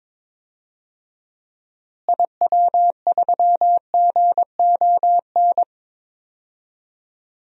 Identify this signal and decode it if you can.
Morse code, heard as a keyed tone, about 22 wpm, 700 Hz. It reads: IW3GON
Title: DXCC-100 Call Signs - 6 Characters - Repeated 3x - 22wpm